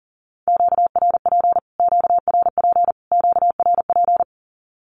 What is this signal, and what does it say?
Morse code, heard as a keyed tone, about 39 wpm, 700 Hz. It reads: QRP QRP QRP
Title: Q-Codes Repeated 3x 40wpm